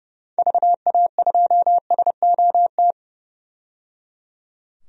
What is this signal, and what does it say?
Morse code, heard as a keyed tone, about 30 wpm, 700 Hz. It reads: VA2SOT
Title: DXCC-100 Call Signs - 6 Characters - 30wpm